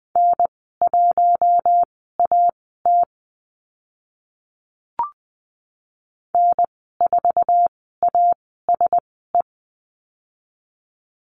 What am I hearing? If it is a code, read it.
Morse code, heard as a keyed tone, about 20 wpm, spaced as for 14 wpm, 700 Hz. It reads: N1AT N4ASE
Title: 048 Lesson 9g - Number 4 - Call Signs 20wpm